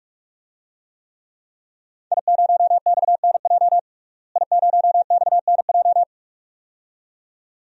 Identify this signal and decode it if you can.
Morse code, heard as a keyed tone, about 46 wpm, 700 Hz. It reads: I0XNJ I0XNJ
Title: DXCC-100 Call Signs - 5 Characters - Repeated 2x - 45wpm